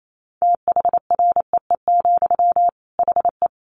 Morse code, heard as a keyed tone, about 28 wpm, 700 Hz. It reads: THREE, HE
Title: Sentences from Top 200 Words 28wpm